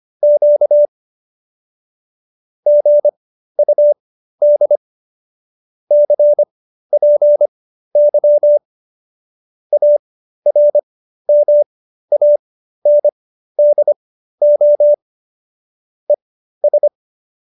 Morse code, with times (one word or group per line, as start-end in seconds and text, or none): 0.2-0.9 Q
2.7-4.8 GUD
5.9-8.6 CPY
9.7-14.9 ARMANDO
16.1-16.9 ES